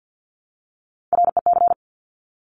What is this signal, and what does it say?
Morse code, heard as a keyed tone, about 50 wpm, 700 Hz. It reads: REC
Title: Top Three Letter Combinations in Words 50wpm